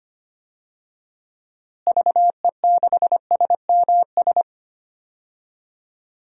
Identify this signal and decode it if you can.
Morse code, heard as a keyed tone, about 25 wpm, 700 Hz. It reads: VE6SMS
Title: DXCC-100 Call Signs - 6 Characters - Repeated 2x - 25wpm